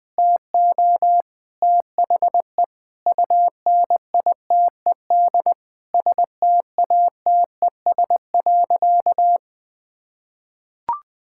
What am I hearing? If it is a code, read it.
Morse code, heard as a keyed tone, about 20 wpm, 700 Hz. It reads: TO THE UNITED STATES.
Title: Sentences with 3-Word Phrases - Spoken Beginning - 20wpm